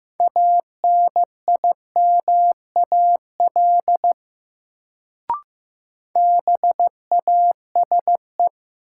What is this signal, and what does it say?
Morse code, heard as a keyed tone, about 15 wpm, 700 Hz. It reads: ANIMAL BASE
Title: Sentences from Top 1000 Words - Review Part 1 - New Word Drill - 15wpm